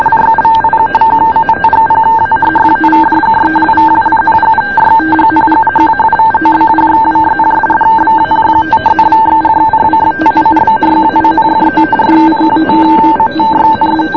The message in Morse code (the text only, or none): none